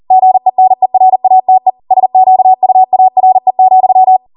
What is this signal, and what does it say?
Morse code, heard as a keyed tone, about 40 wpm, 750 Hz. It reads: GENERATE SQUARE,